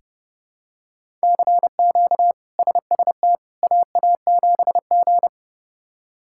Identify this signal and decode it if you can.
Morse code, heard as a keyed tone, about 32 wpm, 700 Hz. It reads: CQ SST AA7G
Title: K1USN SST QSO Practice - 30wpm